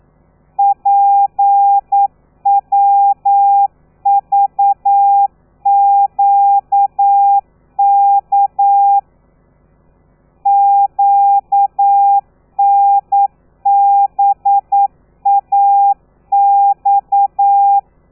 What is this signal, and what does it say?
Morse code, heard as a keyed tone, about 9 wpm, 800 Hz. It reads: PWVQK QNBAX
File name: telegrafia / i2tfj / Lez13 40